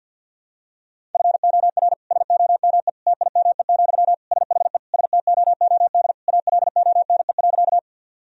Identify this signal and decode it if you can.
Morse code, heard as a keyed tone, about 52 wpm, 700 Hz. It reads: FOR SOME TIME, SHE STOOD ALONE.
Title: Sentences from Top 1000 Words - Review Part 2 - 50wpm